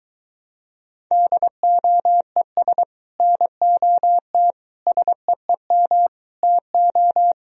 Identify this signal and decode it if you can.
Morse code, heard as a keyed tone, about 24 wpm, 700 Hz. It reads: DOES NOT SEEM TO